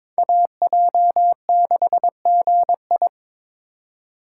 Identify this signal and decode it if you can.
Morse code, heard as a keyed tone, about 21 wpm, 700 Hz. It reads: AJ6GI
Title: USA Call Signs - 4x Character Spacing - 22wpm